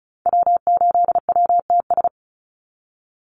Morse code, heard as a keyed tone, about 34 wpm, 700 Hz. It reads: W8WTS
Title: Call Signs Repeated 2x - 35wpm